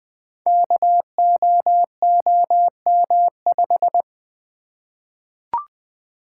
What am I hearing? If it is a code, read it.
Morse code, heard as a keyed tone, about 20 wpm, 700 Hz. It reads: KOOM5
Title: Code Groups - 5 Letter:Numbers - 20wpm